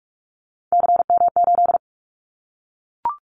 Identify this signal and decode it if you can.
Morse code, heard as a keyed tone, about 44 wpm, 700 Hz. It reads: CM8
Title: Callsign Prefixes for Top 100 DXCC - 45wpm